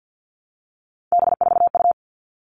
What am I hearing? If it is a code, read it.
Morse code, heard as a keyed tone, about 50 wpm, 700 Hz. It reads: B4U
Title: All Cut Numbers - 3-Digit 50wpm